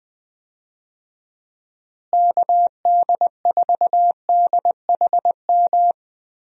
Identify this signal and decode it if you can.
Morse code, heard as a keyed tone, about 20 wpm, 700 Hz. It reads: KD4DHM
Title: USA Call Signs - 6 Characters - Repeated 2x - 20wpm